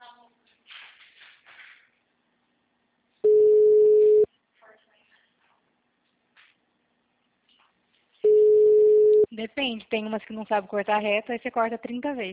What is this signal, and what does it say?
Ringback tone (Europe, 425 Hz), with speech.
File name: out- -2006- - - .73699